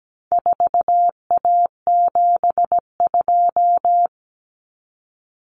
Morse code, heard as a keyed tone, about 17 wpm, 700 Hz. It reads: 4A72
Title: Common Cut Numbers - 4-Digit 17wpm